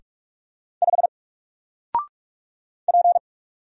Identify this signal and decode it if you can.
Morse code, heard as a keyed tone, about 46 wpm, 700 Hz. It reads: 5 P
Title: Single Letter-Number 45wpm